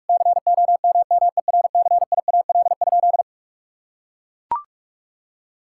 Morse code, heard as a keyed tone, about 41 wpm, 700 Hz. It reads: COMMERCIAL?
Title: Sentences from Top 1000 Words - Review Part 5 - 45wpm